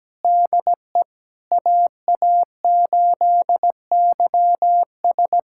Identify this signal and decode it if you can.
Morse code, heard as a keyed tone, about 17 wpm, 700 Hz. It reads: DE AA8YS
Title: USA Callsign DE Callsign 17wpm